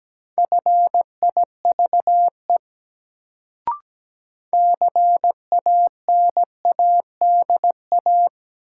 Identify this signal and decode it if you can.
Morse code, heard as a keyed tone, about 17 wpm, 700 Hz. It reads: FIVE CANADA
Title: Sets of 5 Words - New Word Drill - 17wpm